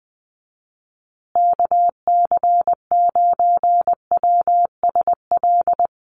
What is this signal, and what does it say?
Morse code, heard as a keyed tone, about 20 wpm, 700 Hz. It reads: KC9WSL
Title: USA Call Signs Repeated 2x 20wpm